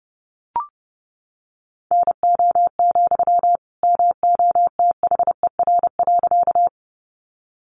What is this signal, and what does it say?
Morse code, heard as a keyed tone, about 30 wpm, 700 Hz. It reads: NO, MOTHER.
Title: Sentences from Top 200 Words 30wpm